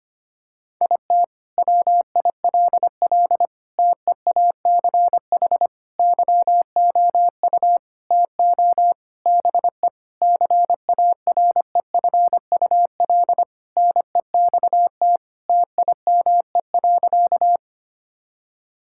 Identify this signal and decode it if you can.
Morse code, heard as a keyed tone, about 25 wpm, 700 Hz. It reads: IT WILL TEACH YOU TO BE CAREFUL NEXT TIME.
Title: Sentences from Top 1600 Words 25wpm